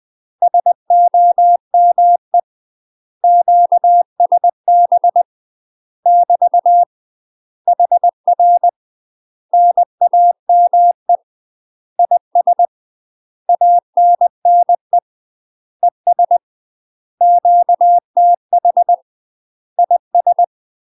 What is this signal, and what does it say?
Morse code, heard as a keyed tone, about 20 wpm, 700 Hz. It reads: SOME QSB <BT> HR NAME IS ANNE ES QTH IS
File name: UK MASTER QSOs TRAINING / 20WPM / UK MASTER QSOs 0063